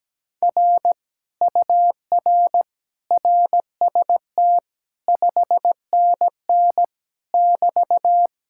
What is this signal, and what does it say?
Morse code, heard as a keyed tone, about 17 wpm, 700 Hz. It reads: R UR RST 5NN <BT>
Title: Speed-Racing Common QSO Phrases - 17wpm